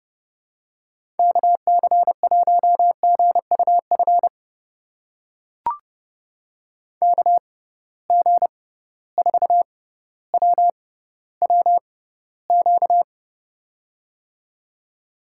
Morse code, heard as a keyed tone, about 30 wpm, spaced as for 15 wpm, 700 Hz. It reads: KC1GUF KG4WWQ
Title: USA Call Signs - 6 Characters - 6x Character Spacing - 30wpm